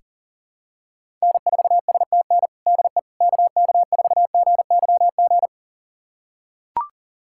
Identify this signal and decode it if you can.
Morse code, heard as a keyed tone, about 40 wpm, 700 Hz. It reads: N4STN DE KK4GYG
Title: USA Callsign DE Callsign 40wpm